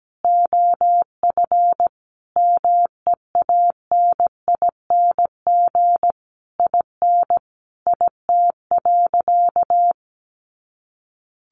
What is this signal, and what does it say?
Morse code, heard as a keyed tone, about 17 wpm, 700 Hz. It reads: OF MEANING IN IT.
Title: Sentences from Top 1200 Words 17wpm